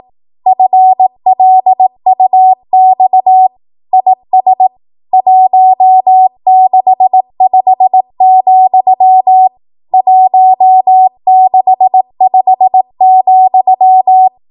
Morse code, heard as a keyed tone, about 18 wpm, 750 Hz. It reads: FLUX IS 165, 165,
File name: Morse / 18WPM